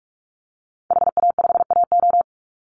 Morse code, heard as a keyed tone, about 44 wpm, 700 Hz. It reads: HA5AO
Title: Call Signs - 45wpm